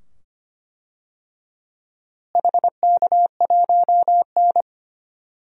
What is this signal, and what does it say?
Morse code, heard as a keyed tone, about 25 wpm, 700 Hz. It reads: HK1N
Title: DXCC-100 Call Signs - 4 Characters - Repeated 2x - 25wpm